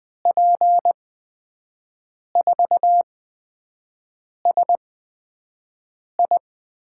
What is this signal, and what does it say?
Morse code, heard as a keyed tone, about 20 wpm, spaced as for 5 wpm, 700 Hz. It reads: P4SI
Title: USA Call Signs - 8x Character Spacing - 20wpm